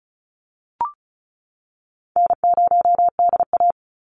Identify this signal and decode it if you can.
Morse code, heard as a keyed tone, about 34 wpm, 700 Hz. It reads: N0DA
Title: NA SOTA Chaser Callsigns 35wpm